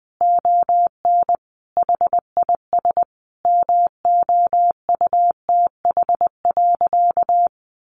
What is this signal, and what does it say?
Morse code, heard as a keyed tone, about 20 wpm, 700 Hz. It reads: ON HIS MOUTH.
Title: Sentences from Top 1500 Words 20wpm